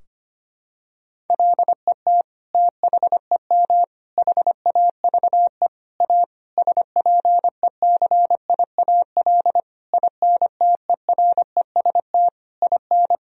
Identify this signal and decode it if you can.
Morse code, heard as a keyed tone, about 25 wpm, 700 Hz. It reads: LET THEM HAVE A SPECIAL INTEREST IN
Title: Sentences from Top 1000 Words - Review Part 3 - 25wpm